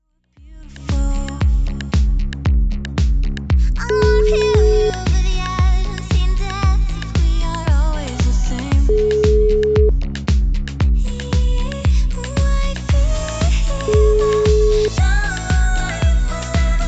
Ringback tone (Europe, 425 Hz), with noise.